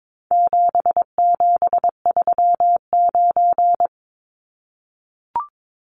Morse code, heard as a keyed tone, about 22 wpm, 700 Hz. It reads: 7739
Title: Numbers - 4-Digit 22wpm